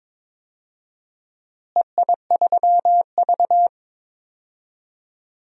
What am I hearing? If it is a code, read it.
Morse code, heard as a keyed tone, about 21 wpm, 700 Hz. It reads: EI3V
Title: DXCC-100 Call Signs - 4 Characters - Repeated 3x - 22wpm